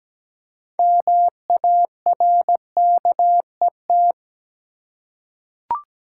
Morse code, heard as a keyed tone, about 17 wpm, 700 Hz. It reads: MARKET
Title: Six Letter Words - 17wpm